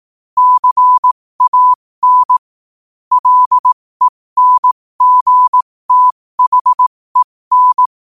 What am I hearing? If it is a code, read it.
Morse code, heard as a keyed tone, about 18 wpm, spaced as for 15 wpm, 1000 Hz. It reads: CAN LENGTHEN
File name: 2024-11-18-15wpm